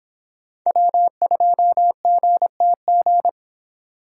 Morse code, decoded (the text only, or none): W2GTG